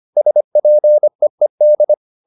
Morse code, heard as a keyed tone, about 25 wpm, 600 Hz. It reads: SPEED